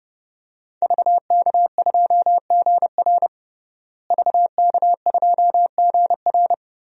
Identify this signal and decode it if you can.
Morse code, heard as a keyed tone, about 30 wpm, 700 Hz. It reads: VK2GR VK2GR
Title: DXCC-100 Call Signs - 5 Characters - Repeated 3x - 30wpm